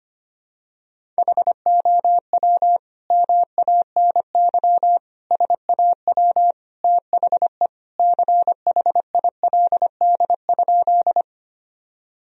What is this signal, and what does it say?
Morse code, heard as a keyed tone, about 25 wpm, 700 Hz. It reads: HOW MANY SAW THE CHILD?
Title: Sentences from Top 1000 Words - Review Part 1 - 25wpm